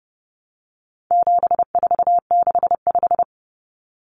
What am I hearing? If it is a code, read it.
Morse code, heard as a keyed tone, about 30 wpm, 700 Hz. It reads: 7465
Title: Numbers - 4-Digit 30wpm